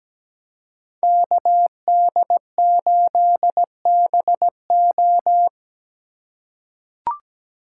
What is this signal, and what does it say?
Morse code, heard as a keyed tone, about 17 wpm, 700 Hz. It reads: KD8BO
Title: USA Call Signs - 5 Characters - 4x Character Spacing - 17wpm